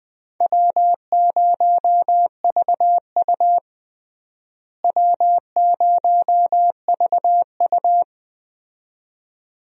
Morse code, heard as a keyed tone, about 21 wpm, 700 Hz. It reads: W0VU W0VU
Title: USA Call Signs - 4 Characters - Repeated 2x - 20wpm